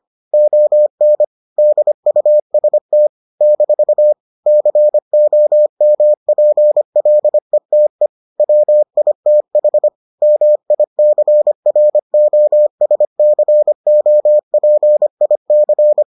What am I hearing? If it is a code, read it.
Morse code, heard as a keyed tone, about 25 wpm, 600 Hz. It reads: ON DUST - COMPLETE WITH MICROSCOPIC